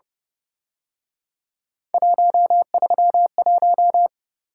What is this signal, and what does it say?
Morse code, heard as a keyed tone, about 31 wpm, 700 Hz. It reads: 131